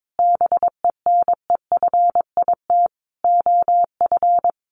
Morse code, heard as a keyed tone, about 22 wpm, 700 Hz. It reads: BENEFIT OF
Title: Sentences from Top 1300 Words 22wpm